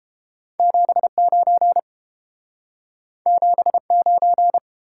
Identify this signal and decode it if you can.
Morse code, heard as a keyed tone, about 30 wpm, 700 Hz. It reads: 79 79